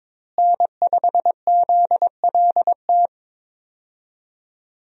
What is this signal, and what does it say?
Morse code, heard as a keyed tone, about 22 wpm, 700 Hz. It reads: N5ZLT
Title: USA Call Signs - 5 Characters - Repeated 2x - 22wpm